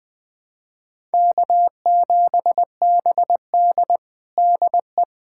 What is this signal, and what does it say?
Morse code, heard as a keyed tone, about 21 wpm, 700 Hz. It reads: K7BD DE
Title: USA Callsign DE Callsign - 4 Characters - 20wpm